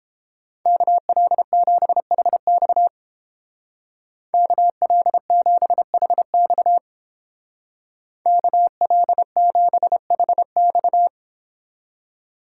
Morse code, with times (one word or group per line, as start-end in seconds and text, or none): 0.7-2.9 KL7HX
4.3-6.8 KL7HX
8.3-11.1 KL7HX